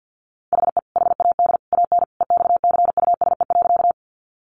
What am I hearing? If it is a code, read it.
Morse code, heard as a keyed tone, about 50 wpm, 700 Hz. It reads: HE HAD AN EXCUSE.
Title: Sentences from Top 800 Words 50wpm